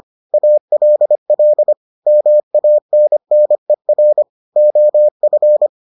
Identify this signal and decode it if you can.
Morse code, heard as a keyed tone, about 25 wpm, 600 Hz. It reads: ALL MANNER OF